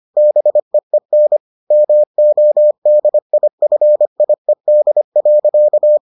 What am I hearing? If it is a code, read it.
Morse code, heard as a keyed tone, about 25 wpm, 600 Hz. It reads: BEEN MODIFIED.